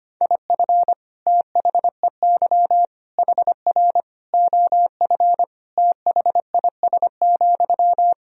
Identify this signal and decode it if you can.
Morse code, heard as a keyed tone, about 25 wpm, 700 Hz. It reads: IF THEY HR OF THIS,
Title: Sentences from Top 400 Words and Abbreviations 25wpm